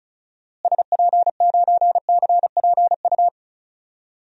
Morse code, heard as a keyed tone, about 36 wpm, 700 Hz. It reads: SP9CPU
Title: DXCC-100 Call Signs - 6 Characters - Repeated 3x - 35wpm